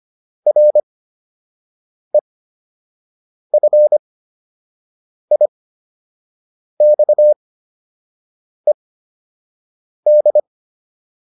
Morse code, heard as a keyed tone, about 25 wpm, spaced as for 6 wpm, 600 Hz. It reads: REFIXED